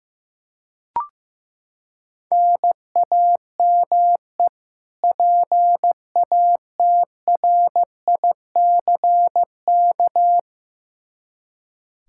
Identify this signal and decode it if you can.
Morse code, heard as a keyed tone, about 15 wpm, 700 Hz. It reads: NAME PATRICK
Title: Practice QSO - Exchange 1 - Beginning - 15wpm